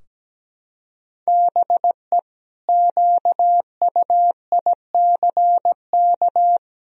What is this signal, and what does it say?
Morse code, heard as a keyed tone, about 17 wpm, 700 Hz. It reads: BE QUICK
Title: Sentences from Top 1000 Words - Review Part 2 - 17wpm